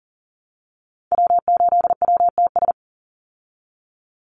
Morse code, heard as a keyed tone, about 40 wpm, 700 Hz. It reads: W8WTS
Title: Call Signs Repeated 2x - 40wpm